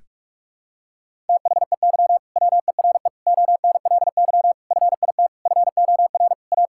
Morse code, heard as a keyed tone, about 44 wpm, 700 Hz. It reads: THEY WERE ONLY FIT FOR A
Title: Sentences from Top 1000 Words - Review Part 5 - 45wpm